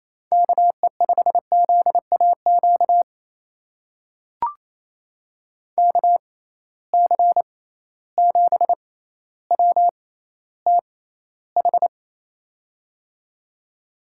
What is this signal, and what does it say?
Morse code, heard as a keyed tone, about 28 wpm, spaced as for 14 wpm, 700 Hz. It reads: KE5ZAQ KC7WTH
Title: USA Call Signs - 6 Characters - 6x Character Spacing - 28wpm